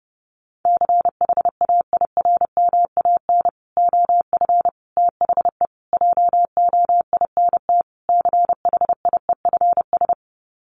Morse code, heard as a keyed tone, about 30 wpm, 700 Hz. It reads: CHAIRMAN OF THE JOINT CHIEFS
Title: Sets of 5 Words Encore 30wpm